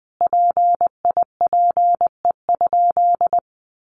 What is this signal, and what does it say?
Morse code, heard as a keyed tone, about 19 wpm, 700 Hz. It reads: PIPE?
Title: Sentences from Top 1500 Words 20wpm